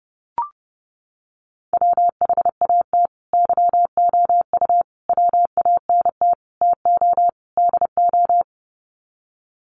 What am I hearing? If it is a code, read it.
Morse code, heard as a keyed tone, about 30 wpm, 700 Hz. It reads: WHAT YOU WANT TO DO
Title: Sets of 5 Words Encore 30wpm